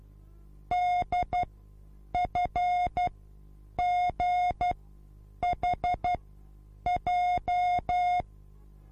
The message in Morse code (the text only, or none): DFGHJ